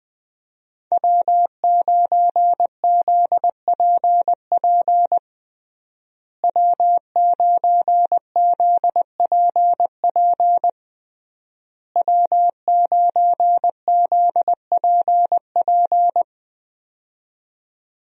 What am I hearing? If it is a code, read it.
Morse code, heard as a keyed tone, about 20 wpm, 700 Hz. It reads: W9ZPP W9ZPP W9ZPP
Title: USA Call Signs - 5 Characters - Repeated 3x - 20wpm